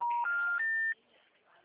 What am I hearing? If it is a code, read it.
Special information tone.